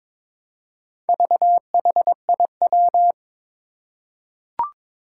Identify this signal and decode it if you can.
Morse code, heard as a keyed tone, about 22 wpm, 700 Hz. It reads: VHIW
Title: Four Challenging Letter-Number Combinations 22wpm